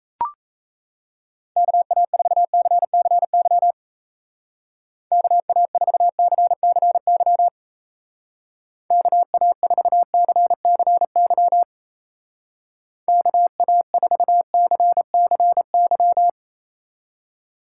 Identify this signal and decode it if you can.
Morse code, heard as a keyed tone, about 35 wpm, 700 Hz. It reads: KA4CCY KA4CCY KA4CCY KA4CCY
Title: Speed Racing - USA 2x3 Call Signs 28wpm